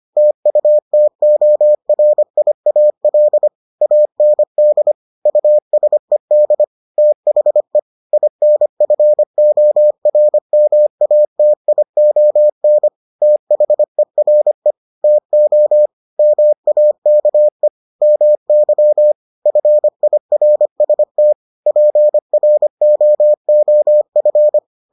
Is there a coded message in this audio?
Yes: Morse code from 0.2 to 24.6 s, heard as a keyed tone, about 25 wpm, 600 Hz. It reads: TUTORIAL AND USED THE INFORMATION THERE TO MAKE MY FIRST PROOF